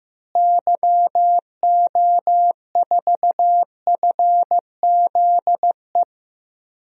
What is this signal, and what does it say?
Morse code, heard as a keyed tone, about 15 wpm, 700 Hz. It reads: YO4FZE
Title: DXCC-100 Call Signs - 6 Characters - Repeated 2x - 15wpm